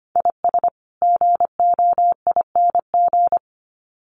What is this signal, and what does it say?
Morse code, heard as a keyed tone, about 25 wpm, 700 Hz. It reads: IS GOING
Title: Sets of 2 Words 25wpm